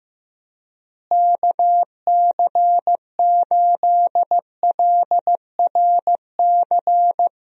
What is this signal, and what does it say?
Morse code, heard as a keyed tone, about 15 wpm, 700 Hz. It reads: KC8LRC
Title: USA Call Signs - 6 Characters - Repeated 3x - 15wpm